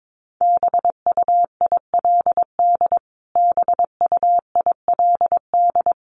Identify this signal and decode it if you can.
Morse code, heard as a keyed tone, about 22 wpm, 700 Hz. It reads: BUILD BUILD
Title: Top 300 Words Repeated 22wpm